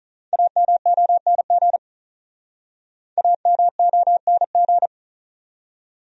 Morse code, heard as a keyed tone, about 47 wpm, 700 Hz. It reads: AMONG AMONG